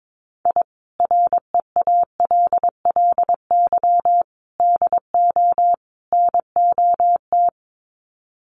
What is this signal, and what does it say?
Morse code, heard as a keyed tone, about 22 wpm, 700 Hz. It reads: I REALLY DO NOT
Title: Sets of 4 Words 22wpm